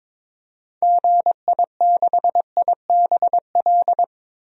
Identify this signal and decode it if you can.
Morse code, heard as a keyed tone, about 21 wpm, 700 Hz. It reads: GI6IBL
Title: DXCC-100 Call Signs - 6 Characters - 22wpm